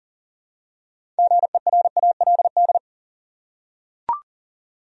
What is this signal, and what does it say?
Morse code, heard as a keyed tone, about 40 wpm, 700 Hz. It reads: GERALD
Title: USA Common Names - 40wpm